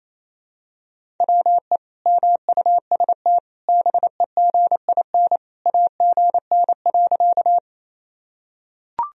Morse code, heard as a keyed tone, about 28 wpm, 700 Hz. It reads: WE MUST BEGIN AGN.
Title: Sentences from Top 300 Words and Abbreviations 28wpm